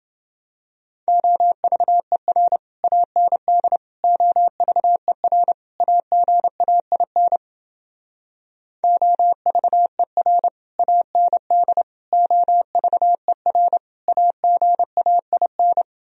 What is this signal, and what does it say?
Morse code, heard as a keyed tone, about 29 wpm, 700 Hz. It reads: OVER AND OVER AGAIN OVER AND OVER AGAIN